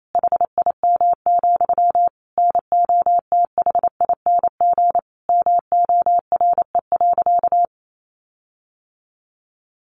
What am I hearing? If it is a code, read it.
Morse code, heard as a keyed tone, about 28 wpm, 700 Hz. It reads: HIM, NOTHING MORE.
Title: Sentences from Top 1300 Words 28wpm